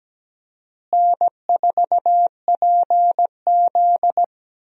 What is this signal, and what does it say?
Morse code, heard as a keyed tone, about 17 wpm, 700 Hz. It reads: N4PZ
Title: USA Call Signs - 4 Characters - Repeated 2x - 17wpm